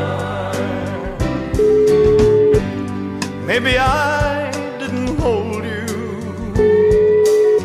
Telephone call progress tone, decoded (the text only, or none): ringback tone